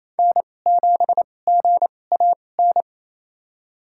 Morse code, heard as a keyed tone, about 28 wpm, spaced as for 20 wpm, 700 Hz. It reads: N7GAN
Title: USA Call Signs - 5 Characters - 2x Character Spacing - 28wpm